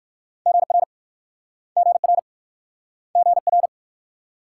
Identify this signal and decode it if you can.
Morse code, heard as a keyed tone, about 56 wpm, 700 Hz. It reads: GR GR GR
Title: Speed-Racing Multi-letter Phonemes - 45wpm